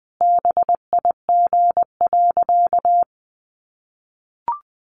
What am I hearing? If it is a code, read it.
Morse code, heard as a keyed tone, about 20 wpm, 700 Hz. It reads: BIG.
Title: Easy Phrases 20wpm